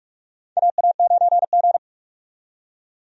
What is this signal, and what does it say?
Morse code, heard as a keyed tone, about 47 wpm, 700 Hz. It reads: AA9G
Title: USA Call Signs - 4 Characters - 45wpm